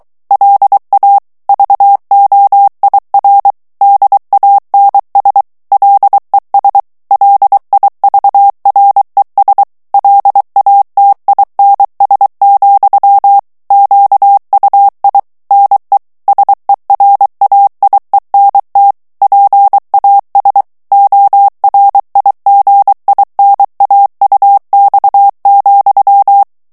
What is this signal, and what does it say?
Morse code, heard as a keyed tone, about 23 wpm, 800 Hz. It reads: LA VOIR DANS LES LIVRES LATINS, QUI NE SERAIENT PAS ORIGINAUX,
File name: morse mp3 / FRANCAIS TEXTE CLAIR 2 VITESSE 22 MOTS MINUTES